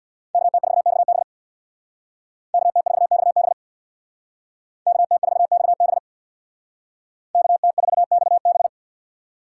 Morse code, heard as a keyed tone, about 64 wpm, 700 Hz. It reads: KT4XB KT4XB KT4XB KT4XB